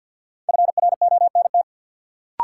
Morse code, heard as a keyed tone, about 51 wpm, 700 Hz. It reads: FRONT
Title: Sentences from Top 500 Words - New Word Drill - 50wpm